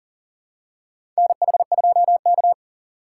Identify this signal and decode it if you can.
Morse code, heard as a keyed tone, about 40 wpm, 700 Hz. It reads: NH2K
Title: USA Call Signs - 4 Characters - 2x Character Spacing - 40wpm